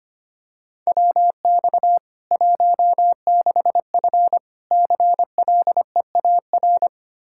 Morse code, heard as a keyed tone, about 25 wpm, 700 Hz. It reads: WX 16F CLEAR
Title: Practice QSO - Exchange 2 - Info - 25wpm